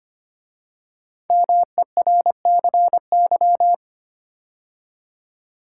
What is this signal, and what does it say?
Morse code, heard as a keyed tone, about 25 wpm, 700 Hz. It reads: MERCY
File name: 5 Letter Words - First 3 Letters Given - 25wpm